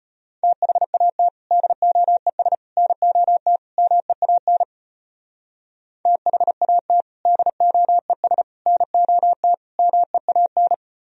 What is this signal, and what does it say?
Morse code, heard as a keyed tone, about 32 wpm, 700 Hz. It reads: THAT DOES NOT MEAN THAT DOES NOT MEAN